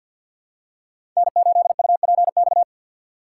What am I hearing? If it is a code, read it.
Morse code, heard as a keyed tone, about 46 wpm, 700 Hz. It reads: N8UPX
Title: USA Call Signs - 5 Characters - 3x Character Spacing - 50wpm